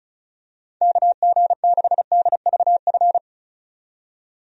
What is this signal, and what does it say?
Morse code, heard as a keyed tone, about 35 wpm, 700 Hz. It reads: KG6DVF
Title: USA Call Signs - 6 Characters - 35wpm